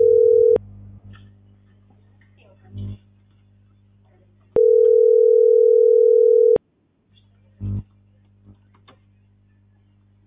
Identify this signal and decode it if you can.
Ringback tone (North America), 440 + 480 Hz.